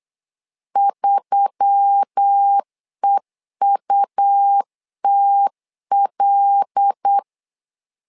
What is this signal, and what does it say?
Morse code, heard as a keyed tone, about 8 wpm, 800 Hz. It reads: 3EUTL
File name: LEZIONI / 5 LBVC / 5 LB 8wpm 2816